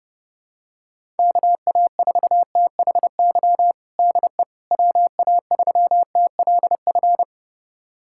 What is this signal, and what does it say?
Morse code, heard as a keyed tone, about 30 wpm, 700 Hz. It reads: KA4THY DE WA3TLF
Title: USA Callsign DE Callsign 30wpm